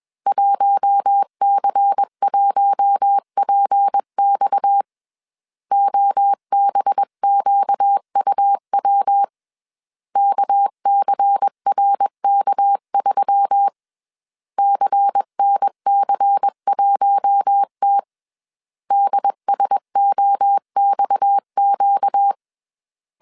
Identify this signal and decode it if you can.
Morse code, heard as a keyed tone, about 21 wpm, 800 Hz. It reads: 1C1PX OBQUW KCRK3 CNC1T DSOXQ
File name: CW / DET / 1 6 1500C S A 20wpm 2048